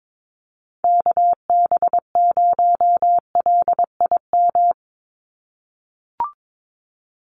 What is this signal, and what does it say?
Morse code, heard as a keyed tone, about 22 wpm, 700 Hz. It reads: KB0LIM
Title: USA Call Signs 22wpm